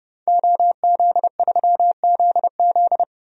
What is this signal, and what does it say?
Morse code, heard as a keyed tone, about 30 wpm, 700 Hz. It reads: OZ3ZZ
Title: DXCC-100 Call Signs - 5 Characters - Repeated 3x - 30wpm